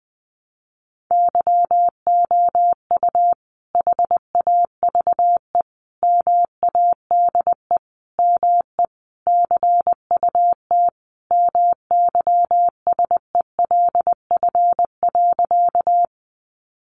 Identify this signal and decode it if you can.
Morse code, heard as a keyed tone, about 20 wpm, 700 Hz. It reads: YOU HAVE MADE ME CUT MYSELF.
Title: Sentences from Top 1300 Words 20wpm